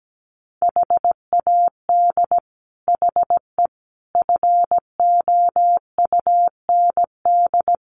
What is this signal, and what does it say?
Morse code, heard as a keyed tone, about 17 wpm, 700 Hz. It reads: HAD HE FOUND
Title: Sentences from Top 1300 Words 17wpm